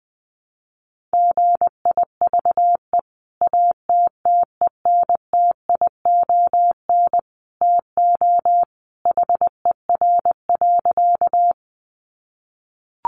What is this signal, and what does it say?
Morse code, heard as a keyed tone, about 20 wpm, 700 Hz. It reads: GIVE ATTENTION TO HER.
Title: Sentences from Top 1000 Words 20wpm